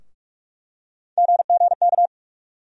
Morse code, heard as a keyed tone, about 47 wpm, 700 Hz. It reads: GGK
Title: USA Call Sign Postfixes - 3 Letters 45wpm